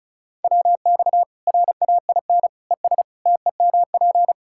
Morse code, heard as a keyed tone, about 36 wpm, 700 Hz. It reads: WX RAIN ES TEMP
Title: K1USN SST QSO Practice 35wpm